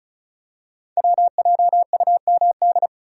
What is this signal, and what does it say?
Morse code, heard as a keyed tone, about 36 wpm, 690 Hz. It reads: WJUMD
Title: Code Groups - 5 Letters - 35wpm